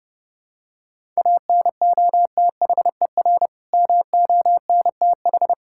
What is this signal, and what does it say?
Morse code, heard as a keyed tone, about 32 wpm, 700 Hz. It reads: ANOTHER MONTH
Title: Sentences from Top 1000 Words - Review Part 5 - 30wpm